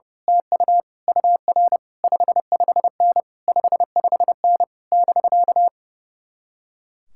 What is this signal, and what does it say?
Morse code, heard as a keyed tone, about 28 wpm, 700 Hz. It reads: TU UR 55N 55N <BK>
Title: Speed-Racing Common QSO Phrases - 30wpm